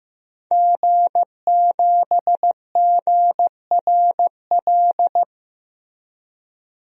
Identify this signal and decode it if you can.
Morse code, heard as a keyed tone, about 15 wpm, 700 Hz. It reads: G7GRL
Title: DXCC-100 Call Signs - 5 Characters - Repeated 2x - 15wpm